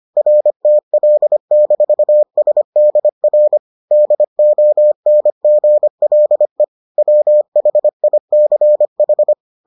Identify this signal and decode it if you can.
Morse code, heard as a keyed tone, about 25 wpm, 600 Hz. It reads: RTL-SDR DONGLE WHICH